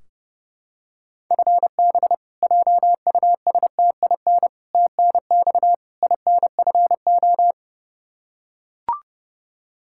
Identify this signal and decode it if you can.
Morse code, heard as a keyed tone, about 30 wpm, 700 Hz. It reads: FB JUSTIN TNX INFO
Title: Practice QSO - Exchange 3 - More Info - 30wpm